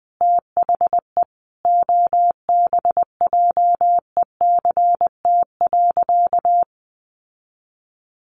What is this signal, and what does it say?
Morse code, heard as a keyed tone, about 20 wpm, 700 Hz. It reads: THE OBJECT.
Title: Sentences from Top 800 Words 20wpm